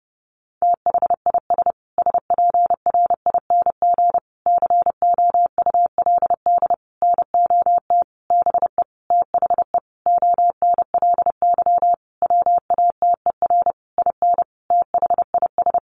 Morse code, heard as a keyed tone, about 30 wpm, 700 Hz. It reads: THIS SPRING COULD NOT BE THE ONLY WATER IN THIS
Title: Sentences from Top 1500 Words 30wpm